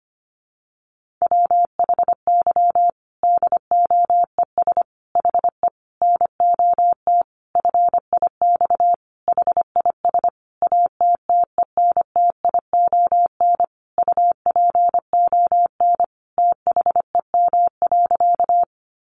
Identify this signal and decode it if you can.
Morse code, heard as a keyed tone, about 25 wpm, 700 Hz. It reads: WHY DOES HE NOT FIX HIS ATTENTION UPON THEM.
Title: Sentences from Top 1400 Words 25wpm